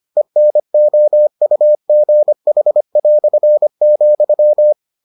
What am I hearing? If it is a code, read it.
Morse code, heard as a keyed tone, about 25 wpm, 600 Hz. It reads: ENOUGH",